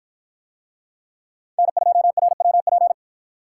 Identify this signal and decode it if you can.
Morse code, heard as a keyed tone, about 49 wpm, 700 Hz. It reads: N2RWP